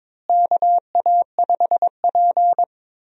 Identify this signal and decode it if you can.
Morse code, heard as a keyed tone, about 22 wpm, 700 Hz. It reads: KA5P
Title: USA Call Signs - 4 Characters - Repeated 2x - 22wpm